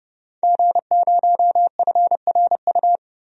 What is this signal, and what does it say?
Morse code, heard as a keyed tone, about 30 wpm, 700 Hz. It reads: G0FRU
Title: DXCC-100 Call Signs - 5 Characters - 30wpm